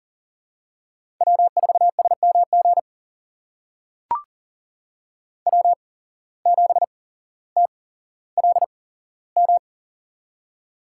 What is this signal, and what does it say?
Morse code, heard as a keyed tone, about 43 wpm, spaced as for 17 wpm, 700 Hz. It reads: W4SMG W7TLM
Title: USA Call Signs - 5 Characters - 8x Character Spacing - 40wpm